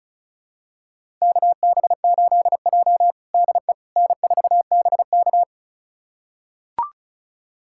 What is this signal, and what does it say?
Morse code, heard as a keyed tone, about 36 wpm, 700 Hz. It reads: KB8J DE N4BK
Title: USA Callsign DE Callsign - 4 Characters - 35wpm